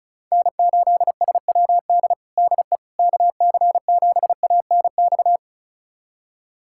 Morse code, heard as a keyed tone, about 34 wpm, 700 Hz. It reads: N8SWD DE KC7ANX
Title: USA Callsign DE Callsign 35wpm